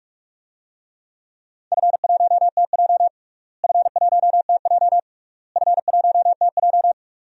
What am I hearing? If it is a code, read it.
Morse code, heard as a keyed tone, about 41 wpm, 700 Hz. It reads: F1TJ F1TJ F1TJ
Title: DXCC-100 Call Signs - 4 Characters - Repeated 3x - 45wpm